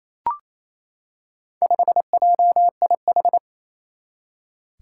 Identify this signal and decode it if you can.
Morse code, heard as a keyed tone, about 28 wpm, 700 Hz. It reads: 5JIH
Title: Four Challenging Letter-Number Combinations 28wpm